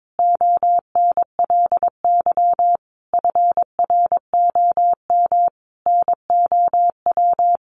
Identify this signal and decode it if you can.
Morse code, heard as a keyed tone, about 22 wpm, 700 Hz. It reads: ONLY FROM NOW
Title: Easy Phrases 22wpm